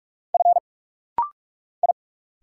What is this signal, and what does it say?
Morse code, heard as a keyed tone, about 46 wpm, 700 Hz. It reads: F I
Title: Single Letters - Rapid-Fire 45wpm